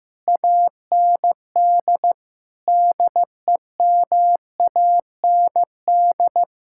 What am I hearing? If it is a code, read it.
Morse code, heard as a keyed tone, about 15 wpm, 700 Hz. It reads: AND DEMAND
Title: Sentences from Top 1600 Words 15wpm